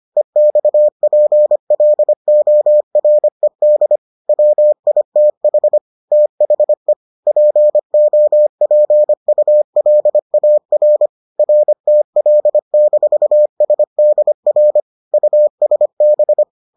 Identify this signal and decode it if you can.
Morse code, heard as a keyed tone, about 25 wpm, 600 Hz. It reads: EXPLORED WITH THE POPULAR RTL-SDR USB